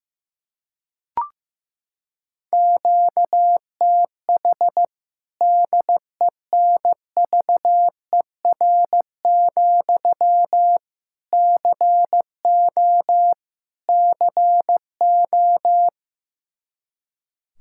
Morse code, heard as a keyed tone, about 15 wpm, 700 Hz. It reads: QTH DENVER, CO CO
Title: K1USN SST QSO Practice 15wpm